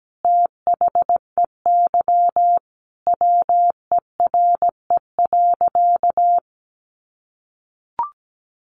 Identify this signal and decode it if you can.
Morse code, heard as a keyed tone, about 17 wpm, 700 Hz. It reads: THEY WERE.
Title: Sentences from Top 900 Words 17wpm